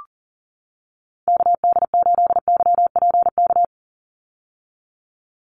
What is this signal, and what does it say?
Morse code, heard as a keyed tone, about 39 wpm, 700 Hz. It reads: KD8YPK
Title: USA Call Signs 40wpm